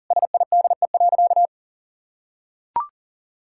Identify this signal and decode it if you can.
Morse code, heard as a keyed tone, about 40 wpm, 700 Hz. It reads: SIDE.
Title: Sentences from Top 1000 Words - Review Part 6 - 40wpm